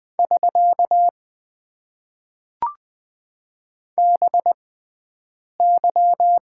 Morse code, heard as a keyed tone, about 20 wpm, spaced as for 7 wpm, 700 Hz. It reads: <SK> BY
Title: Top 90 QSO Elements - 6x Character Spacing - 20wpm